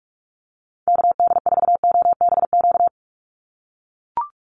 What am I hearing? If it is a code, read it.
Morse code, heard as a keyed tone, about 44 wpm, 700 Hz. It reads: KD4OBQ
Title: USA Call Signs 45wpm